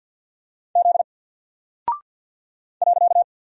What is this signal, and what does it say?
Morse code, heard as a keyed tone, about 50 wpm, 700 Hz. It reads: Z .
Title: Single Character (Letter:Number:Punctuation:Prosign) 50wpm